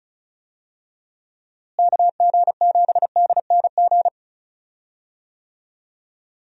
Morse code, heard as a keyed tone, about 34 wpm, 700 Hz. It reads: KG7DNG
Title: USA Call Signs - 6 Characters - Repeated 2x - 35wpm